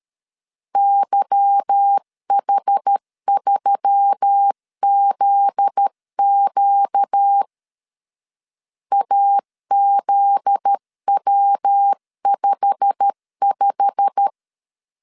Morse code, heard as a keyed tone, about 13 wpm, 800 Hz. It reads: YH3ZQ AZW55
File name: LEZIONI / 8 YJ45 / 8 YJ45 12wpm 2151